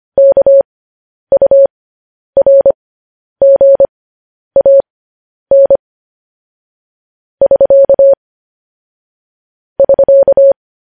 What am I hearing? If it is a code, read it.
Morse code, heard as a keyed tone, about 25 wpm, spaced as for 10 wpm, 550 Hz. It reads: KURGAN <SK> <SK>